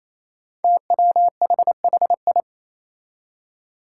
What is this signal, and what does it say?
Morse code, heard as a keyed tone, about 27 wpm, 700 Hz. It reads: TWHHI
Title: Five Challenging Letter-Number Combinations 28wpm